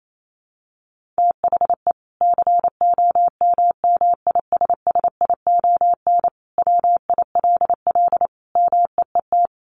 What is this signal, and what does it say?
Morse code, heard as a keyed tone, about 28 wpm, 700 Hz. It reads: THE COMMISSION WILL MEET
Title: Sentences from Top 1400 Words 28wpm